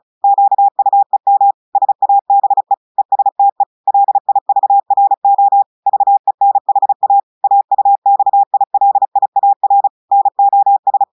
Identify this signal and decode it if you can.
Morse code, heard as a keyed tone, about 35 wpm, 800 Hz. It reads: QUEM SABE ESTE LIVRO VENHA AUXILIAR NOS
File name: audios / portugais / portugais texte en clair 1 vitesse 35 mots minutes